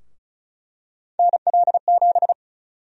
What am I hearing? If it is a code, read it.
Morse code, heard as a keyed tone, about 34 wpm, 700 Hz. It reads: NL7
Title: USA Call Sign Prefixes - 35wpm